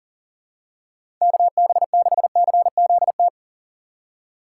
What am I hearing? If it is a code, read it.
Morse code, heard as a keyed tone, about 38 wpm, 700 Hz. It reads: KB6CZT
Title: USA Call Signs - 6 Characters - 4x Character Spacing - 40wpm